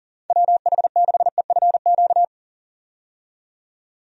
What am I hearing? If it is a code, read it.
Morse code, heard as a keyed tone, about 40 wpm, 690 Hz. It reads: WH6EFQ
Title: USA Call Signs - 6 Characters - 40wpm